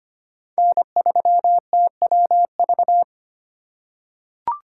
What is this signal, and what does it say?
Morse code, heard as a keyed tone, about 25 wpm, 700 Hz. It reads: N3TWV
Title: USA Call Signs - 2x Character Spacing - 25wpm